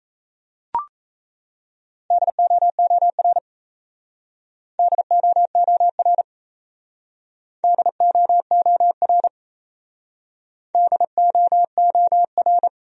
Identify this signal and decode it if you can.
Morse code, heard as a keyed tone, about 36 wpm, 700 Hz. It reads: DOOR DOOR DOOR DOOR